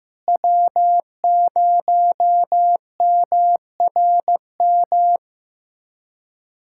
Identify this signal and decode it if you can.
Morse code, heard as a keyed tone, about 15 wpm, 700 Hz. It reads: W0MRM
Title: USA Callsign DE Callsign - 5 Characters - 15wpm